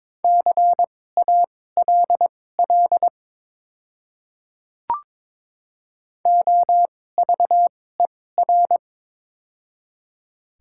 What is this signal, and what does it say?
Morse code, heard as a keyed tone, about 22 wpm, spaced as for 16 wpm, 700 Hz. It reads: CALL OVER
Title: 2x Character Spacing - Top 100 Words - 22wpm